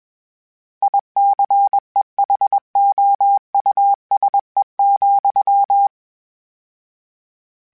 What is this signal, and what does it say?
Morse code, heard as a keyed tone, about 21 wpm, 800 Hz. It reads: ICEHOUSE,
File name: General / General-mp3s / Houses @20